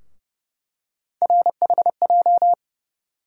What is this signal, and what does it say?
Morse code, heard as a keyed tone, about 30 wpm, 700 Hz. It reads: RHJ
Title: USA Call Sign Postfixes - 3 Letters 30wpm